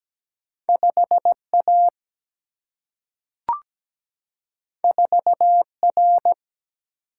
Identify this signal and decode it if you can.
Morse code, heard as a keyed tone, about 17 wpm, 700 Hz. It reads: 5A 4R
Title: Single-Letter Single-Number 17wpm